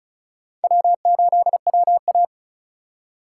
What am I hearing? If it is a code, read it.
Morse code, heard as a keyed tone, about 48 wpm, spaced as for 42 wpm, 700 Hz. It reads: W8WA
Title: Speed Racing - USA Call Signs 35wpm